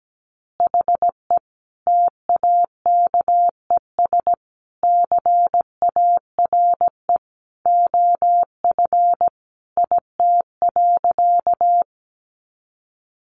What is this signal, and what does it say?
Morse code, heard as a keyed tone, about 17 wpm, 700 Hz. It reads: HE TAKES CARE OF IT.
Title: Additional Easy Phrases 17wpm